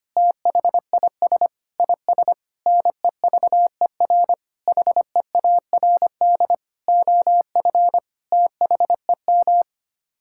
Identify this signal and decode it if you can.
Morse code, heard as a keyed tone, about 25 wpm, 700 Hz. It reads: THIS IS NEVER HEARD OF THEM
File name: Sets of 9 Words - 25wpm